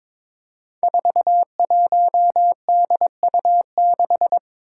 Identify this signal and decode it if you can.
Morse code, heard as a keyed tone, about 22 wpm, 700 Hz. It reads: 41DU6
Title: Five Challenging Letter-Number Combinations 22wpm